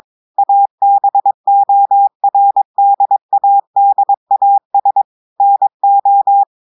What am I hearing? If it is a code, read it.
Morse code, heard as a keyed tone, about 22 wpm, 800 Hz. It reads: ABORDADAS NO
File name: audios / portugais / portugais texte en clair 2 vitesse 22 mots minutes